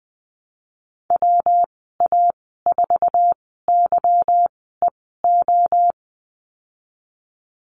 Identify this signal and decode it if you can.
Morse code, heard as a keyed tone, about 19 wpm, 700 Hz. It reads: W A 4 Y E O
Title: 154 Lesson 25g - Letter p - Call Signs 20wpm